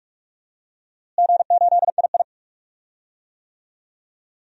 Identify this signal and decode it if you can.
Morse code, heard as a keyed tone, about 41 wpm, 700 Hz. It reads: G8II
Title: DXCC-100 Call Signs - 4 Characters - Repeated 2x - 45wpm